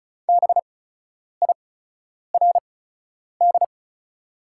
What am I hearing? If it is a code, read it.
Morse code, heard as a keyed tone, about 34 wpm, spaced as for 10 wpm, 690 Hz. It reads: BIRD
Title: 8x Character Spacing - Top 500 Words - 35wpm